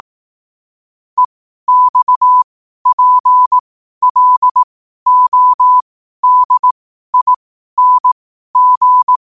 Morse code, heard as a keyed tone, about 18 wpm, spaced as for 12 wpm, 1000 Hz. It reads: EXPLODING
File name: qotdfiles / 12wpm